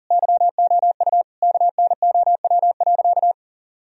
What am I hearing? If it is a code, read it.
Morse code, heard as a keyed tone, about 40 wpm, 700 Hz. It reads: YOU KNOW.
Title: Sentences from Top 1000 Words - Review Part 2 - 40wpm